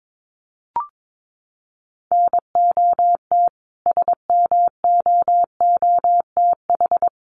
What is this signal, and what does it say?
Morse code, heard as a keyed tone, about 22 wpm, 700 Hz. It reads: NOT SMOOTH
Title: Sentences from Top 1400 Words 22wpm